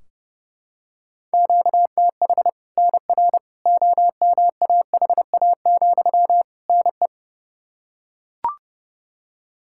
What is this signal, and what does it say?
Morse code, heard as a keyed tone, about 30 wpm, 700 Hz. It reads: QTH NR OMAHA, NE
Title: Practice QSO - Exchange 1 - Beginning - 30wpm